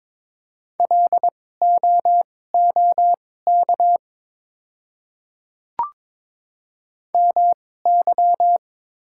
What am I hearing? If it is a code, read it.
Morse code, heard as a keyed tone, about 22 wpm, spaced as for 16 wpm, 700 Hz. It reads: LOOK MY
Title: 2x Character Spacing - Top 100 Words - 22wpm